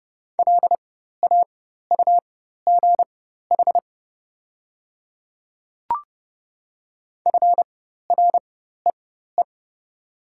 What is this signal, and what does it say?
Morse code, heard as a keyed tone, about 30 wpm, spaced as for 14 wpm, 690 Hz. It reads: LAUGH FREE
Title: 4x Character Spacing - Top 500 Words - 30wpm